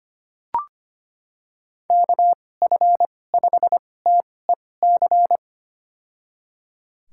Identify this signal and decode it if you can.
Morse code, heard as a keyed tone, about 25 wpm, spaced as for 18 wpm, 700 Hz. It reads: KF5TEC
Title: USA Call Signs - 6 Characters - 2x Character Spacing - 25wpm